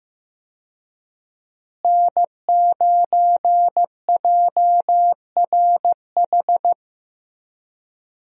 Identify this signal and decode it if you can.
Morse code, heard as a keyed tone, about 15 wpm, 700 Hz. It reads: N9JRH
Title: USA Call Signs - 5 Characters - Repeated 2x - 15wpm